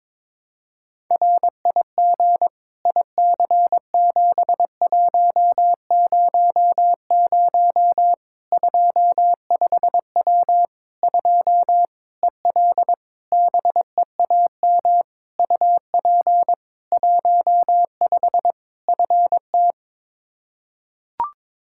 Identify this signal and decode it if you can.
Morse code, heard as a keyed tone, about 22 wpm, 700 Hz. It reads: RIG IC7100 25W 2 EL BEAM UP 15 FT
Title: K1USN SST QSO Practice 22wpm